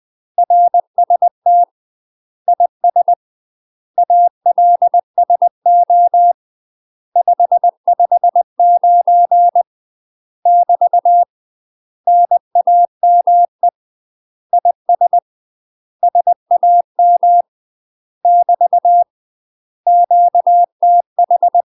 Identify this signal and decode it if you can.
Morse code, heard as a keyed tone, about 20 wpm, 700 Hz. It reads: RST IS ALSO 559 <BT> NAME IS SAM <BT> QTH
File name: UK MASTER QSOs TRAINING / 20WPM / UK MASTER QSOs 0005